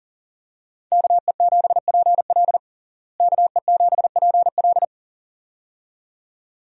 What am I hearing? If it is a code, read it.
Morse code, heard as a keyed tone, about 40 wpm, 700 Hz. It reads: KE7PL KE7PL
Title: USA Call Signs - 5 Characters - Repeated 2x - 40wpm